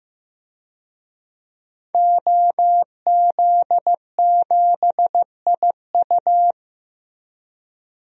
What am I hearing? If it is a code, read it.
Morse code, heard as a keyed tone, about 15 wpm, 700 Hz. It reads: OZ7IU
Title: DXCC-100 Call Signs - 5 Characters - Repeated 2x - 15wpm